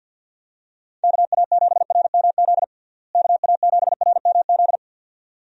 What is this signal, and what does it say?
Morse code, heard as a keyed tone, about 50 wpm, 700 Hz. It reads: KA7RMZ KA7RMZ
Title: USA Call Signs - 6 Characters - Repeated 2x - 50wpm